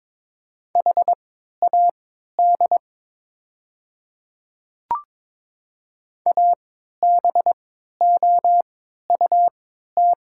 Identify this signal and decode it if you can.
Morse code, heard as a keyed tone, about 22 wpm, spaced as for 12 wpm, 700 Hz. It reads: HAD ABOUT
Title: 3x Character Spacing - Top 100 Words - 22wpm